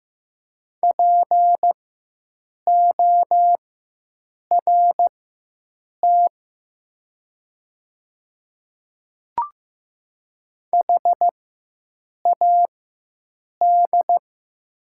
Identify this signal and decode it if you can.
Morse code, heard as a keyed tone, about 15 wpm, spaced as for 7 wpm, 700 Hz. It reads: PORT HAD
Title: 4x Character Spacing - Top 300 Words - 15wpm